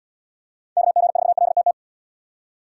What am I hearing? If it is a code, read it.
Morse code, heard as a keyed tone, about 69 wpm, 700 Hz. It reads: KG4LTT